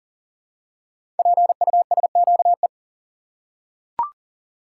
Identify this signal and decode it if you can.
Morse code, heard as a keyed tone, about 40 wpm, 700 Hz. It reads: PUSQE
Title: Code Groups - 5 Letters - 40wpm